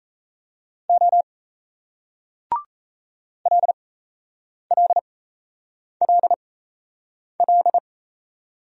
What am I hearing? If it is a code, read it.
Morse code, heard as a keyed tone, about 38 wpm, spaced as for 8 wpm, 700 Hz. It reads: O LLLL